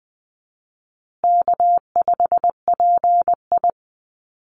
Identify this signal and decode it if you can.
Morse code, heard as a keyed tone, about 20 wpm, 700 Hz. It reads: K5PI
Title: Easy Call Signs Repeated 3x - 20wpm